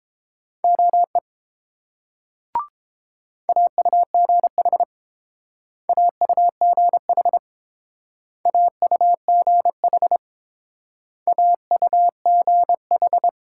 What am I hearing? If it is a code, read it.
Morse code, heard as a keyed tone, about 29 wpm, 700 Hz. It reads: OE AUGH AUGH AUGH AUGH